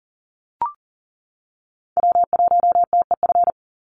Morse code, heard as a keyed tone, about 40 wpm, 700 Hz. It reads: W1TEF
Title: Call Signs Repeated 2x - 40wpm